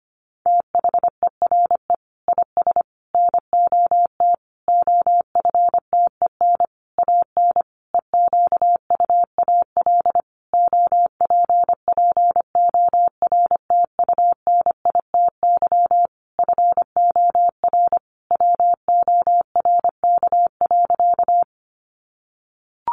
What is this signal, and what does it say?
Morse code, heard as a keyed tone, about 25 wpm, 700 Hz. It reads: THERE IS NOT OFTEN AN EQUAL OPPORTUNITY FOR WORK.
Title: Sentences from Top 1400 Words 25wpm